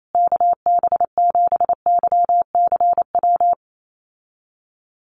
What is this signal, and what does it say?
Morse code, heard as a keyed tone, about 28 wpm, 700 Hz. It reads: KB7YCW
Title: USA Call Signs Repeated 2x 28wpm